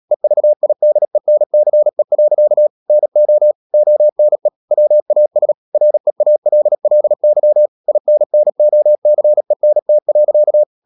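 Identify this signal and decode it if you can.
Morse code, heard as a keyed tone, about 37 wpm, 600 Hz. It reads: EVIDENCE. NO ONE WAS REALLY INNOCENT.
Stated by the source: Morse code practice at 35–39 wpm sending English text of Ken Follett's novel Never, from CW med Gnister.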